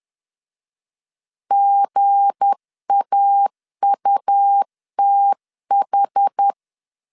Morse code, heard as a keyed tone, about 10 wpm, 800 Hz. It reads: GAUTH
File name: LEZIONI / 7 GF9Z / 7 G 10wpm 2501